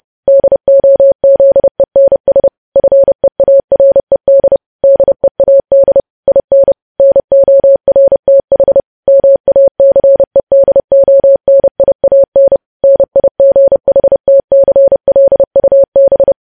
Morse code, heard as a keyed tone, about 30 wpm, 550 Hz. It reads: DOZENS FEARED DEAD IN NORTH MACEDONIAN NIGHTCLUB
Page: News Headlines in Morse Code at 30 WPM